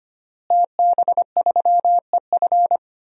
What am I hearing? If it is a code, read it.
Morse code, heard as a keyed tone, about 25 wpm, 700 Hz. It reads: TB3EF
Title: DXCC-100 Call Signs - 5 Characters - Repeated 3x - 25wpm